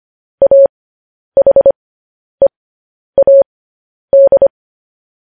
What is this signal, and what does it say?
Morse code, heard as a keyed tone, about 25 wpm, spaced as for 10 wpm, 550 Hz. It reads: AHEAD
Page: news-headlines-in-morse-code-at-10-wpm